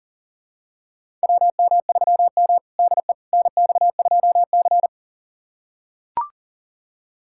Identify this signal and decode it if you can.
Morse code, heard as a keyed tone, about 40 wpm, 700 Hz. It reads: WM3M DE NX2C
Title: USA Callsign DE Callsign - 4 Characters - 40wpm